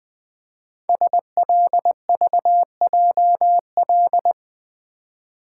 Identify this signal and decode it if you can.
Morse code, heard as a keyed tone, about 20 wpm, 700 Hz. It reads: SLVJL
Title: Code Groups - 5 Letters - 20wpm